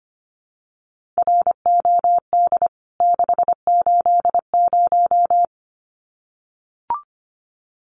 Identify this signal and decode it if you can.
Morse code, heard as a keyed tone, about 25 wpm, 700 Hz. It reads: ROD 680
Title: CWT Exchange 25wpm